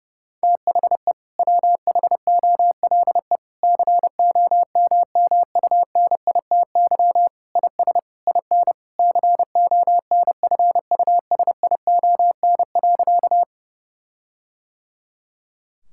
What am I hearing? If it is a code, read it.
Morse code, heard as a keyed tone, about 30 wpm, 700 Hz. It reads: THE WHOLE COMMUNITY IS IN CONFUSION.
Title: Sentences from Top 1600 Words 30wpm